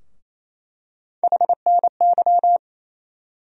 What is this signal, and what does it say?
Morse code, heard as a keyed tone, about 28 wpm, 700 Hz. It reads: HNY
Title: USA Call Sign Postfixes - 3 Letters 28wpm